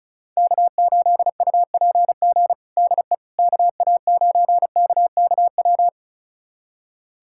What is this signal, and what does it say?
Morse code, heard as a keyed tone, about 36 wpm, 700 Hz. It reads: K8UPG DE KA9KKW
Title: USA Callsign DE Callsign 35wpm